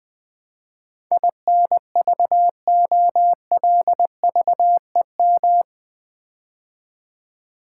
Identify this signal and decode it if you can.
Morse code, heard as a keyed tone, about 20 wpm, 700 Hz. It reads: INVOLVEM
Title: Build A Word - 4 Letters At A Time - 11-Letter Words - 20wpm